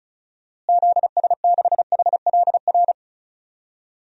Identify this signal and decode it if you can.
Morse code, heard as a keyed tone, about 36 wpm, 700 Hz. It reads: ZS6HLR
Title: DXCC-100 Call Signs - 6 Characters - Repeated 3x - 35wpm